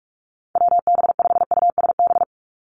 Morse code, heard as a keyed tone, about 44 wpm, 700 Hz. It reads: WB5USB
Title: NA SOTA Chaser Callsigns 45wpm